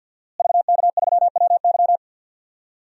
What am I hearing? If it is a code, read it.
Morse code, heard as a keyed tone, about 51 wpm, 700 Hz. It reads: VK3WY
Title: DXCC-100 Call Signs - 5 Characters - Repeated 2x - 50wpm